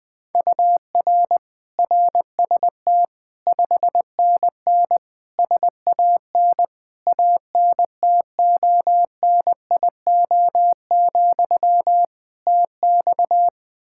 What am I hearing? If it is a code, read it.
Morse code, heard as a keyed tone, about 20 wpm, 700 Hz. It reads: UR RST 5NN SAN ANTONIO, TX
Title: Practice QSO - Exchange 1 - Beginning - 20wpm